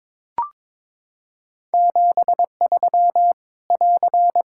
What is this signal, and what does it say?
Morse code, heard as a keyed tone, about 22 wpm, 700 Hz. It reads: 73 <AR>
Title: Practice QSO - Exchange 4 - Ending - 22wpm